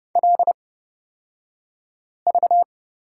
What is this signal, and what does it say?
Morse code, heard as a keyed tone, about 30 wpm, spaced as for 5 wpm, 700 Hz. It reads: LV
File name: Lesson 36d - BT - Two Characters 30wpm